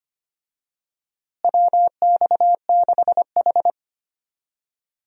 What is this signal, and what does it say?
Morse code, heard as a keyed tone, about 25 wpm, 700 Hz. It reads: WX6H
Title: USA Call Signs - 4 Characters - Repeated 2x - 25wpm